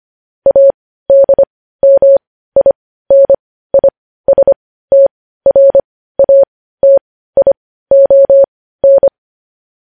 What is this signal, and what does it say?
Morse code, heard as a keyed tone, about 25 wpm, spaced as for 15 wpm, 550 Hz. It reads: ADMINISTRATION
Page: news headlines in morse code at 15 wpm